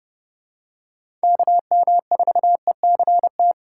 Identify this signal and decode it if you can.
Morse code, heard as a keyed tone, about 30 wpm, 700 Hz. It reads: KM4ECT
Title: USA Call Signs - 3x Character Spacing - 30wpm